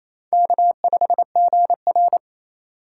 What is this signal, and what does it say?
Morse code, heard as a keyed tone, about 28 wpm, 700 Hz. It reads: K5GR
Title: USA Call Signs - 4 Characters - 4x Character Spacing - 28wpm